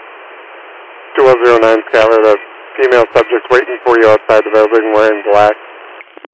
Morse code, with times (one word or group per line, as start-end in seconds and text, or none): none